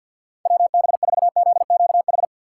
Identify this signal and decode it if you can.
Morse code, heard as a keyed tone, about 48 wpm, 700 Hz. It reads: WB4ZQH
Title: USA Call Signs - 6 Characters - 4x Character Spacing - 50wpm